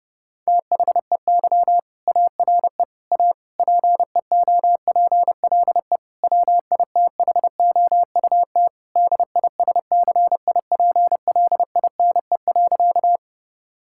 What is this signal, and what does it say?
Morse code, heard as a keyed tone, about 30 wpm, 700 Hz. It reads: THEY ARE A PEOPLE WITHOUT DISCIPLINE.
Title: Sentences from Top 1000 Words - Review Part 6 - 30wpm